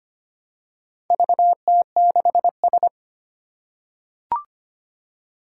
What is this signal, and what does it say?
Morse code, heard as a keyed tone, about 25 wpm, 700 Hz. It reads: VT6S
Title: Four Challenging Letter-Number Combinations 25wpm